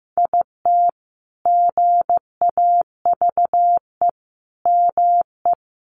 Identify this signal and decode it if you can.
Morse code, heard as a keyed tone, about 14 wpm, 700 Hz. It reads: IT GAVE ME
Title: Sentences from Top 1300 Words 15wpm